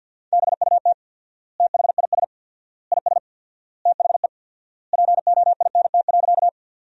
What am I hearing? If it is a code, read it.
Morse code, heard as a keyed tone, about 50 wpm, 700 Hz. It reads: BUT THIS IS THE POINT.
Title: 4x Word-Spacing - Sentences from Top 200 Words - 50wpm